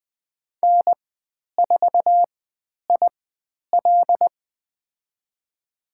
Morse code, heard as a keyed tone, about 20 wpm, spaced as for 10 wpm, 700 Hz. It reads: N4IL
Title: 102 Lesson 17g - Letter u - Call Signs 20wpm